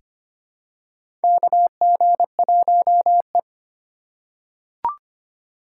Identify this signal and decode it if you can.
Morse code, heard as a keyed tone, about 25 wpm, 700 Hz. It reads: KG1E
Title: USA Call Signs - 3x Character Spacing - 25wpm